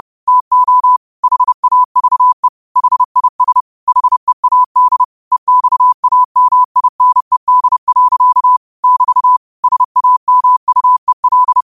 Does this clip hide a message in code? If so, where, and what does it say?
Yes: Morse code from 0.3 to 11.6 s, heard as a keyed tone, about 30 wpm, 1000 Hz. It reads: TO HAVE HIS HEAD EXAMINED. <BT> SAMUEL